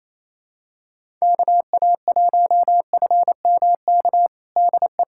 Morse code, heard as a keyed tone, about 28 wpm, 700 Hz. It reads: KA1FMK DE
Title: USA Callsign DE Callsign - 6 Characters - 28wpm